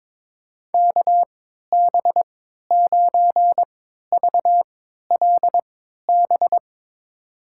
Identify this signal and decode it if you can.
Morse code, heard as a keyed tone, about 22 wpm, spaced as for 13 wpm, 700 Hz. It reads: KB9VLB
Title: USA Call Signs - 6 Characters - 3x Character Spacing - 22wpm